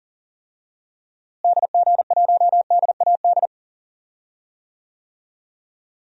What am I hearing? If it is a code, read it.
Morse code, heard as a keyed tone, about 40 wpm, 700 Hz. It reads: DG1DAD
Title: DXCC-100 Call Signs - 6 Characters - Repeated 2x - 40wpm